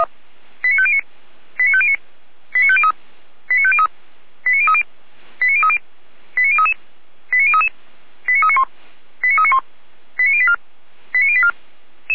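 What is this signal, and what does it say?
Five-tone selective call (ZVEI), heard as quick runs of five tones, tones of about 70 ms. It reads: 89509 89509 89503 89503 89030 89030 89033 89033 89391 89391 89084 89084
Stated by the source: ZVEI five-tone selective call 89080, received via SDR.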